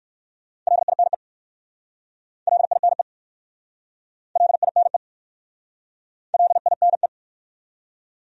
Morse code, heard as a keyed tone, about 59 wpm, 700 Hz. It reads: LINE LINE LINE LINE